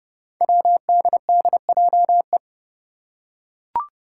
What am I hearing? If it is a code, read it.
Morse code, heard as a keyed tone, about 29 wpm, 700 Hz. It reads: WDDJE
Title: Five Challenging Letter-Number Combinations 30wpm